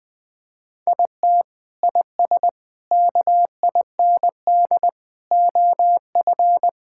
Morse code, heard as a keyed tone, about 20 wpm, 700 Hz. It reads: IT IS KIND OF
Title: Sentences with 2-Word Phrases - 20wpm